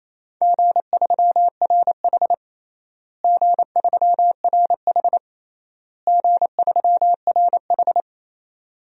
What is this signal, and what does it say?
Morse code, heard as a keyed tone, about 28 wpm, 700 Hz. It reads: G3RH G3RH G3RH
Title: DXCC-100 Call Signs - 4 Characters - Repeated 3x - 28wpm